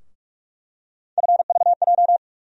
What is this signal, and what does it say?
Morse code, heard as a keyed tone, about 47 wpm, 700 Hz. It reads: FVJ
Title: USA Call Sign Postfixes - 3 Letters 45wpm